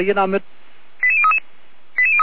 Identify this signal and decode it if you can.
Five-tone selective call (ZVEI), heard as quick runs of five tones, tones of about 70 ms, with speech. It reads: 88033